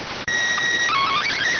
SSTV (slow-scan television), PD 290.